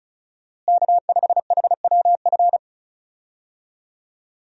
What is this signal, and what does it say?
Morse code, heard as a keyed tone, about 34 wpm, 700 Hz. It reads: K5HWF
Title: USA Call Signs - 5 Characters - Repeated 3x - 35wpm